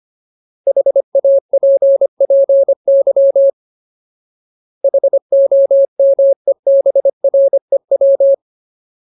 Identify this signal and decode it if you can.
Morse code, heard as a keyed tone, about 25 wpm, 550 Hz. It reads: HAPPY HOMEBREW
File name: Ultimate Word List / 25WPM Ultimate